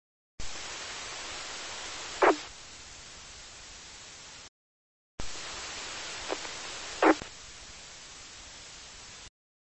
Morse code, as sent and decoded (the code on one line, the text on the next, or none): none